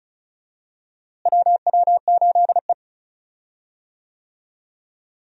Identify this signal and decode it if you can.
Morse code, heard as a keyed tone, about 36 wpm, 700 Hz. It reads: WW8E
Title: USA Call Signs - 4 Characters - Repeated 3x - 35wpm